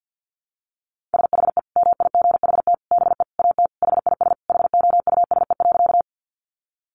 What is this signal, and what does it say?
Morse code, heard as a keyed tone, about 50 wpm, 700 Hz. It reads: SHE MIGHT BE AT HIS HOUSE.
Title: Sentences from Top 300 Words 50wpm